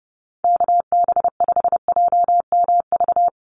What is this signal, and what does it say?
Morse code, heard as a keyed tone, about 30 wpm, 700 Hz. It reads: KB5JMV
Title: USA Call Signs 30wpm